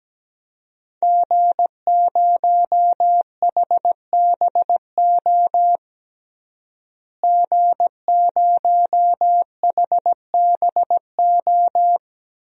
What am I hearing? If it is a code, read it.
Morse code, heard as a keyed tone, about 17 wpm, 700 Hz. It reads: G0HBO G0HBO
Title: DXCC-100 Call Signs - 5 Characters - Repeated 3x - 17wpm